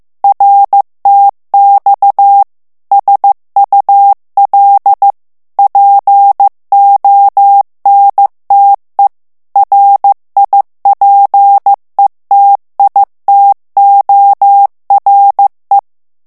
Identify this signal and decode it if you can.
Morse code, heard as a keyed tone, about 15 wpm, 800 Hz. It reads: RTX SUL PONTE RIPETITORE
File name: ITALIEN texte clair 1 vitesse 14 mots minute